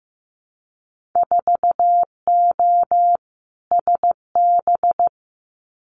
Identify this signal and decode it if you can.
Morse code, heard as a keyed tone, about 15 wpm, 700 Hz. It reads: 4O SB
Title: Winter Field Day Exchanges 15wpm